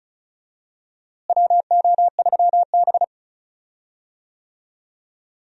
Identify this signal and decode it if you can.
Morse code, heard as a keyed tone, about 36 wpm, 700 Hz. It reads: WO3B
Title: USA Call Signs - 4 Characters - Repeated 3x - 35wpm